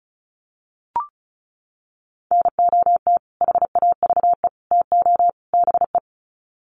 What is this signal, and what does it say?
Morse code, heard as a keyed tone, about 34 wpm, 700 Hz. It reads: NOT HAVE TO BE
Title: Sets of 4 Words 35wpm